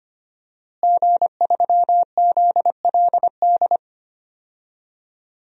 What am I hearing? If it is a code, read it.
Morse code, heard as a keyed tone, about 25 wpm, 700 Hz. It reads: G3ZLD
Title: DXCC-100 Call Signs - 5 Characters - Repeated 3x - 25wpm